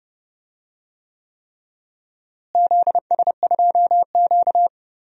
Morse code, heard as a keyed tone, about 31 wpm, 700 Hz. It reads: ZS2Q
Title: DXCC-100 Call Signs - 4 Characters - Repeated 3x - 30wpm